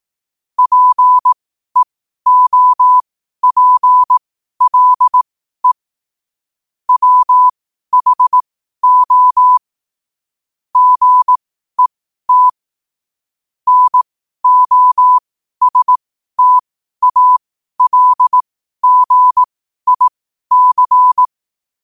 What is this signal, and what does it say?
Morse code, heard as a keyed tone, about 18 wpm, spaced as for 12 wpm, 1000 Hz. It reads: PEOPLE WHO GET NOSTALGIC
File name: qotdfiles / 12wpm